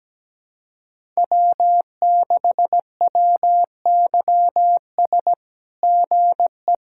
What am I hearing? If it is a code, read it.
Morse code, heard as a keyed tone, about 17 wpm, 700 Hz. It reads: W6WYS GE
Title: SOTA QSO Practice 17wpm